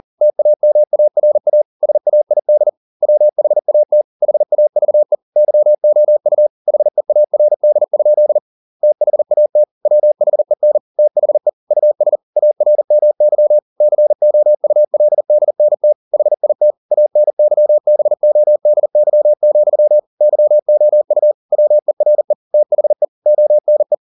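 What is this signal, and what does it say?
Morse code, heard as a keyed tone, about 40 wpm, 600 Hz. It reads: TAMARA SAID WHAT HAVE YOU HEARD? THAT WHEN THE US ARMY COULDNT HIT ANYBODY, YOU WERE THE ONE
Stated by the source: Morse code at 40 wpm, practice text, CW med Gnister.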